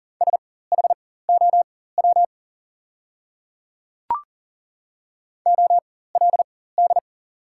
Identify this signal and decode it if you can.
Morse code, heard as a keyed tone, about 40 wpm, spaced as for 19 wpm, 690 Hz. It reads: SHOW OLD
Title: 4x Character Spacing - Top 200 Words - 40wpm